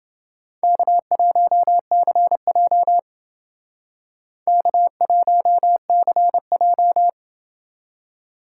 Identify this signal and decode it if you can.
Morse code, heard as a keyed tone, about 29 wpm, 700 Hz. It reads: K1CJ K1CJ